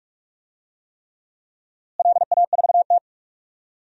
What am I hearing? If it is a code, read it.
Morse code, heard as a keyed tone, about 46 wpm, 700 Hz. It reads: LA4T
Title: DXCC-100 Call Signs - 4 Characters - Repeated 3x - 45wpm